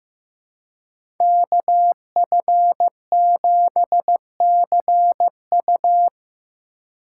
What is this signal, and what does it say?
Morse code, heard as a keyed tone, about 15 wpm, 700 Hz. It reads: KF7CU
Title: USA Call Signs - 5 Characters - 2x Character Spacing - 15wpm